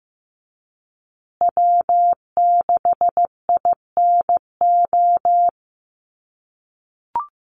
Morse code, heard as a keyed tone, about 15 wpm, 700 Hz. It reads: W6INO
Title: USA Call Signs Repeated 2x 15wpm